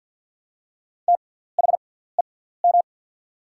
Morse code, heard as a keyed tone, about 48 wpm, spaced as for 17 wpm, 700 Hz. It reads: THEM
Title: 6x Character Spacing - Top 100 Words - 50wpm